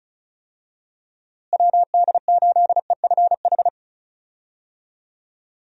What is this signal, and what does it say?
Morse code, heard as a keyed tone, about 34 wpm, 700 Hz. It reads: WD8EFH
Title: USA Call Signs - 6 Characters - Repeated 3x - 35wpm